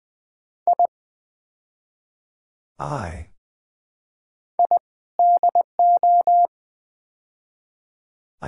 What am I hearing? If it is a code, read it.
Morse code, heard as a keyed tone, about 20 wpm, 700 Hz. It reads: I I DO
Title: Build A Phrase - 5-Word Phrases - 20wpm